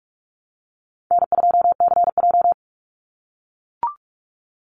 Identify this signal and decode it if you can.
Morse code, heard as a keyed tone, about 44 wpm, 700 Hz. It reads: N2CJ
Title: Easy Call Signs - 45wpm